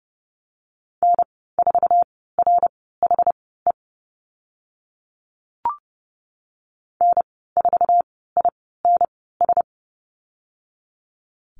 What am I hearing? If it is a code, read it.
Morse code, heard as a keyed tone, about 30 wpm, spaced as for 17 wpm, 700 Hz. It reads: N4RHE N4INS
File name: Lesson 14g - Number 2 - Call Signs 30wpm